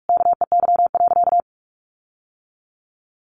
Morse code, heard as a keyed tone, about 44 wpm, 700 Hz. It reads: KEY.
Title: Sentences from Top 1400 Words 45wpm